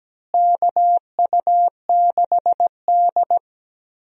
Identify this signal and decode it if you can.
Morse code, heard as a keyed tone, about 17 wpm, 700 Hz. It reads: KU6D
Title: USA Call Signs - 4 Characters - Repeated 3x - 17wpm